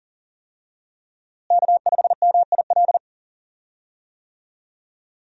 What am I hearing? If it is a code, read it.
Morse code, heard as a keyed tone, about 37 wpm, 700 Hz. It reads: K5MIL
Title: USA Call Signs - 5 Characters - Repeated 2x - 40wpm